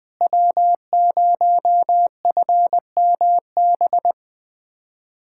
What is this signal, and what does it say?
Morse code, heard as a keyed tone, about 20 wpm, 700 Hz. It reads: W0FMB
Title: USA Callsign DE Callsign - 5 Characters - 20wpm